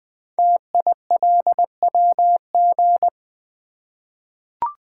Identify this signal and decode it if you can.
Morse code, heard as a keyed tone, about 20 wpm, 700 Hz. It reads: TILWG
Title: Code Groups - 5 Letters - 20wpm